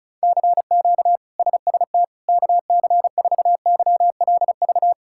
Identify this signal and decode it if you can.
Morse code, heard as a keyed tone, about 34 wpm, 700 Hz. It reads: CQ SST KC4YLV
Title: K1USN SST QSO Practice - 35wpm